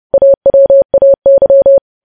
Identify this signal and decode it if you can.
Morse code, heard as a keyed tone, about 30 wpm, 550 Hz. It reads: AWAY
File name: podcast / News headlines-1x-30x30